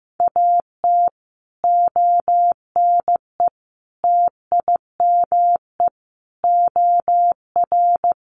Morse code, heard as a keyed tone, about 15 wpm, 700 Hz. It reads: AT ONE TIME OR
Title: Sets of 5 Words Encore 15wpm